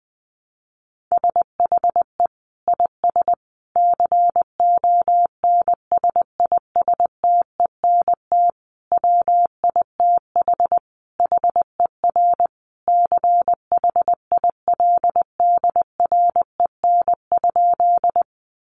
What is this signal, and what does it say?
Morse code, heard as a keyed tone, about 20 wpm, 700 Hz. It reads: SHE IS CONSISTENT WITH HER CHILDREN?
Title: Sentences from Top 1500 Words 20wpm